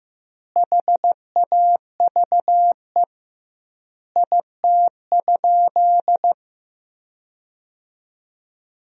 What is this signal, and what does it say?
Morse code, heard as a keyed tone, about 15 wpm, 700 Hz. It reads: HAVE IT?
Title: 2x Word-Spacing - Sentences from Top 100 Words - 15wpm